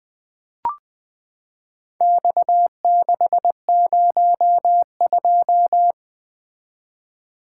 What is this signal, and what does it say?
Morse code, heard as a keyed tone, about 20 wpm, 700 Hz. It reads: X602
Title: Four Letter-Number Combinations - 20wpm